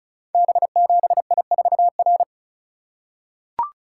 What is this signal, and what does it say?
Morse code, heard as a keyed tone, about 36 wpm, 700 Hz. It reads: B7I4R
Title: Code Groups - 5 Letter:Numbers - 35wpm